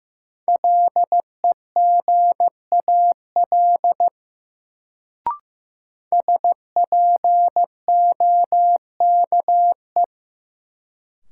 Morse code, heard as a keyed tone, about 15 wpm, 700 Hz. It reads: LEGAL SPOKE
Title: Sentences from Top 1200 Words - New Word Drill Alt - 15wpm